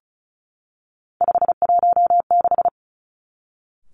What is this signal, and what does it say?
Morse code, heard as a keyed tone, about 36 wpm, 700 Hz. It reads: 516
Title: Three Challenging Letter-Number Combinations 35wpm